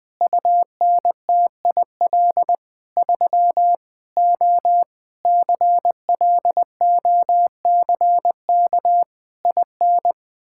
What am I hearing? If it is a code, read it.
Morse code, heard as a keyed tone, about 21 wpm, 700 Hz. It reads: UNTIL 3 O CLOCK IN
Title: Pangram Challenge v2 - 20wpm